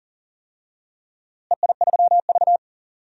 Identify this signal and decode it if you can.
Morse code, heard as a keyed tone, about 40 wpm, 700 Hz. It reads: EI3V
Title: DXCC-100 Call Signs - 4 Characters - Repeated 3x - 40wpm